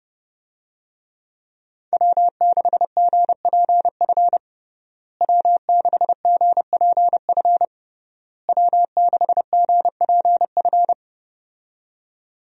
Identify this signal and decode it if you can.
Morse code, heard as a keyed tone, about 32 wpm, 700 Hz. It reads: W6GPF W6GPF W6GPF
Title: USA Call Signs - 5 Characters - Repeated 3x - 30wpm